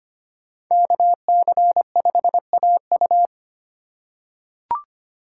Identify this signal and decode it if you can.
Morse code, heard as a keyed tone, about 25 wpm, 700 Hz. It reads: KC5AU
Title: USA Call Signs - 5 Characters - 4x Character Spacing - 25wpm